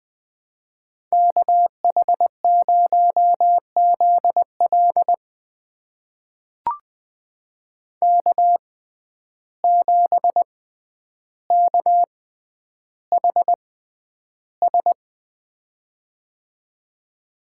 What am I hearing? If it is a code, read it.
Morse code, heard as a keyed tone, about 21 wpm, spaced as for 10 wpm, 700 Hz. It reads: KH0ZL K7KHS
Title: USA Call Signs - 5 Characters - 6x Character Spacing - 20wpm